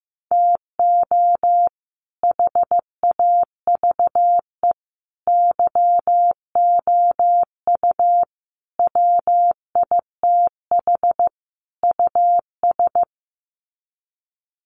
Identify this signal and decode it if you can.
Morse code, heard as a keyed tone, about 15 wpm, 700 Hz. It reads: TO HAVE YOU WITH US
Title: Sets of 5 Words 15wpm